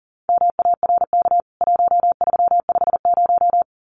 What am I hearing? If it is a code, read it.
Morse code, heard as a keyed tone, about 40 wpm, 700 Hz. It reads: MARK 1350
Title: CWT Exchange 40wpm